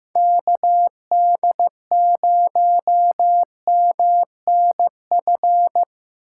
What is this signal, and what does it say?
Morse code, heard as a keyed tone, about 17 wpm, 700 Hz. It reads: KD0MNF
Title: Speed Racing - USA Call Signs 15wpm